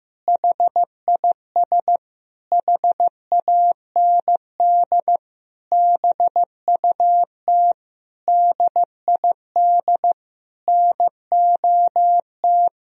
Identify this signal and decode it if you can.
Morse code, heard as a keyed tone, about 16 wpm, 700 Hz. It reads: HIS HAND BUT DID NOT
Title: Sentences from Top 1000 Words - Review Part 1 - 15wpm